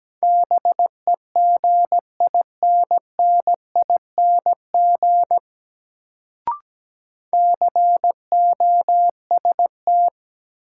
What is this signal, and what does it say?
Morse code, heard as a keyed tone, about 17 wpm, 700 Hz. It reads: BEGINNING COST
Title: Sets of 3 Words - New Word Drill - 17wpm